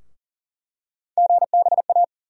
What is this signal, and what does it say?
Morse code, heard as a keyed tone, about 40 wpm, 700 Hz. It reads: GBA
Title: USA Call Sign Postfixes - 3 Letters 40wpm